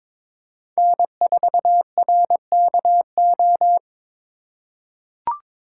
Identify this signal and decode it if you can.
Morse code, heard as a keyed tone, about 22 wpm, 700 Hz. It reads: N4RKO
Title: USA Call Signs - 5 Characters - 4x Character Spacing - 22wpm